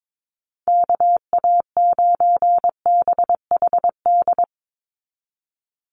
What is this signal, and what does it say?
Morse code, heard as a keyed tone, about 22 wpm, 700 Hz. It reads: KA9BHD
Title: Call Signs Repeated 2x - 22wpm